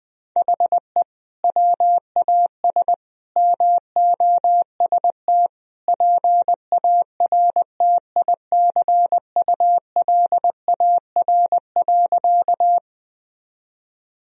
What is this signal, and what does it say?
Morse code, heard as a keyed tone, about 20 wpm, 700 Hz. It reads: HE WAS MOST PARTICULAR.
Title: Sentences from Top 1000 Words - Review Part 4 - 20wpm